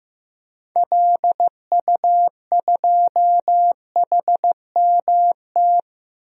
Morse code, heard as a keyed tone, about 15 wpm, 690 Hz. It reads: LU2HMT
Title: DXCC-100 Call Signs - 6 Characters - 15wpm